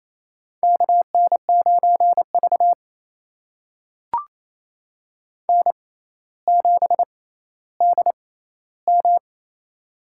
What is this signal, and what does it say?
Morse code, heard as a keyed tone, about 27 wpm, spaced as for 14 wpm, 700 Hz. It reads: KN9V N7DM
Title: USA Call Signs - 4 Characters - 6x Character Spacing - 28wpm